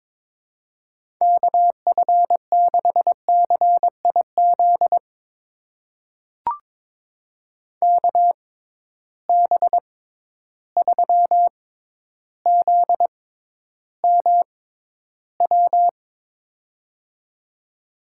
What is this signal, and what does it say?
Morse code, heard as a keyed tone, about 22 wpm, spaced as for 11 wpm, 700 Hz. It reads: KF6CIZ KB3ZMW
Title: USA Call Signs - 6 Characters - 6x Character Spacing - 22wpm